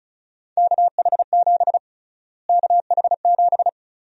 Morse code, heard as a keyed tone, about 34 wpm, 700 Hz. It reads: KH7 KH7
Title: USA Call Sign Prefixes - Repeated 2x - 35wpm